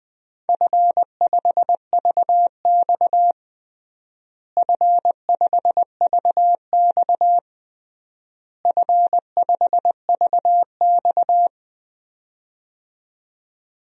Morse code, heard as a keyed tone, about 21 wpm, 700 Hz. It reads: F5VX F5VX F5VX
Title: DXCC-100 Call Signs - 4 Characters - Repeated 3x - 20wpm